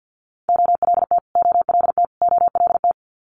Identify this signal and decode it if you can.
Morse code, heard as a keyed tone, about 50 wpm, 700 Hz. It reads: OLT OLT OLT
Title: Multi-letter Phonemes Repeated 3x 50wpm